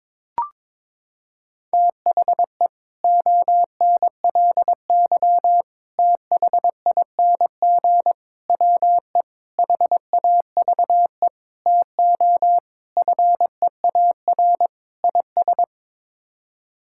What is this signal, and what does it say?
Morse code, heard as a keyed tone, about 22 wpm, 700 Hz. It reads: THE ONLY THING WE HAVE TO FEAR IS
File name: practice / 22wpm / Sets of 8 Words - 22wpm